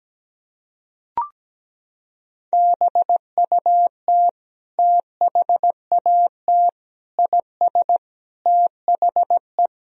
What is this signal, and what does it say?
Morse code, heard as a keyed tone, about 17 wpm, 700 Hz. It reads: BUT THAT IS THE
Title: Sentences from Top 1000 Words - Review Part 3 - 17wpm